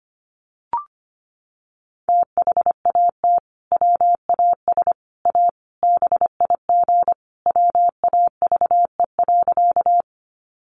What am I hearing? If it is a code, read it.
Morse code, heard as a keyed tone, about 25 wpm, 700 Hz. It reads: THAT WAS A BIG WAVE.
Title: Sentences from Top 1400 Words 25wpm